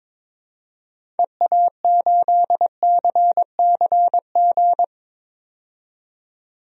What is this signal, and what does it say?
Morse code, heard as a keyed tone, about 22 wpm, 700 Hz. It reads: EA8CCG
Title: DXCC-100 Call Signs - 6 Characters - 22wpm